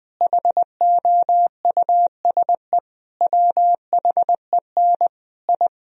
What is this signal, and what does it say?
Morse code, heard as a keyed tone, about 20 wpm, 700 Hz. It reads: HOUSE WHEN I
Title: Sentences from Top 1000 Words - Review Part 4 - 20wpm